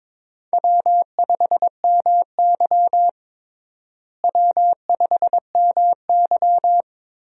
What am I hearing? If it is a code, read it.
Morse code, heard as a keyed tone, about 21 wpm, 700 Hz. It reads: W5MY W5MY
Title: USA Call Signs - 4 Characters - Repeated 3x - 22wpm